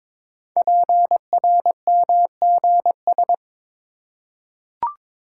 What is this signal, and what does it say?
Morse code, heard as a keyed tone, about 21 wpm, 700 Hz. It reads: PRMGS
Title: Code Groups - 5 Letters - 22wpm